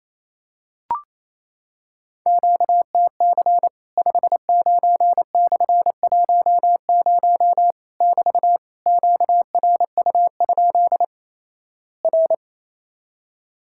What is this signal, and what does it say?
Morse code, heard as a keyed tone, about 28 wpm, 700 Hz. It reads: QTC 59/10 <BT> QRU? R
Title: Worked All Europe DX Contest - QTC Exchange - 28wpm